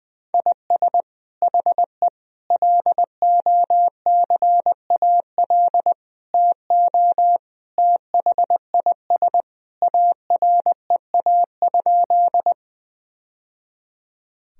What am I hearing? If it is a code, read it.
Morse code, heard as a keyed tone, about 20 wpm, 700 Hz. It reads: IS HE LOCAL TO THIS AREA?
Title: Sentences from Top 1000 Words - Review Part 3 - 20wpm